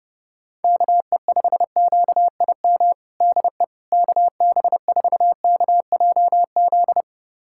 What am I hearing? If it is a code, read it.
Morse code, heard as a keyed tone, about 30 wpm, 700 Hz. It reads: KE5QIM DE KB4KJZ
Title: USA Callsign DE Callsign - 6 Characters - 30wpm